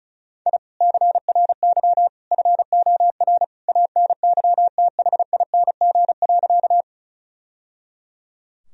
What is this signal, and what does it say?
Morse code, heard as a keyed tone, about 35 wpm, 700 Hz. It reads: I CRY FOR ANYTHING.
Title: Sentences from Top 1000 Words - Review Part 2 - 35wpm